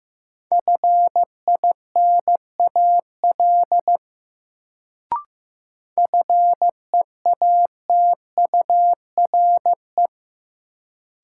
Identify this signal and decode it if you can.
Morse code, heard as a keyed tone, about 16 wpm, 700 Hz. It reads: FINAL FEATURE
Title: Sentences from Top 800 Words - New Word Drill - 15wpm